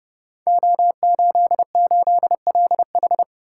Morse code, heard as a keyed tone, about 30 wpm, 700 Hz. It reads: O88LH
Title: Code Groups - 5 Letter:Numbers - 30wpm